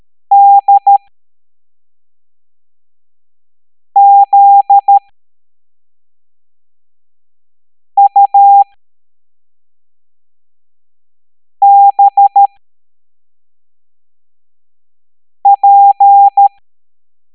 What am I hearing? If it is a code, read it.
Morse code, heard as a keyed tone, about 13 wpm, spaced as for 3 wpm, 800 Hz. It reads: DZUBP
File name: hamdata / L07-EX02-HQP